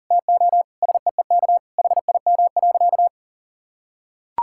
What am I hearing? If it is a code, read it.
Morse code, heard as a keyed tone, about 40 wpm, 700 Hz. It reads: TO SEEK HIM.
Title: Sentences from Top 1600 Words 40wpm